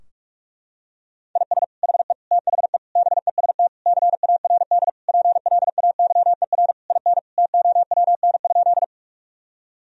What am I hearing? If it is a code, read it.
Morse code, heard as a keyed tone, about 41 wpm, 700 Hz. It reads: IS HE THE BEST CARD PLAYER IN TOWN?
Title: Sentences from Top 1000 Words - Review Part 4 - 45wpm